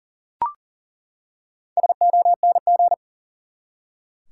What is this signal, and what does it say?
Morse code, heard as a keyed tone, about 40 wpm, 700 Hz. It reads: SONG
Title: Four Letter Words - 40wpm